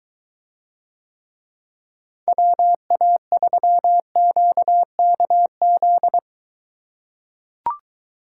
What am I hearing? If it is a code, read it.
Morse code, heard as a keyed tone, about 23 wpm, 700 Hz. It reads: WA3QKZ